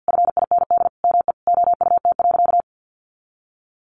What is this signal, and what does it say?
Morse code, heard as a keyed tone, about 50 wpm, 700 Hz. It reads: FIND ME OUT.
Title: Sentences from Top 500 Words 50wpm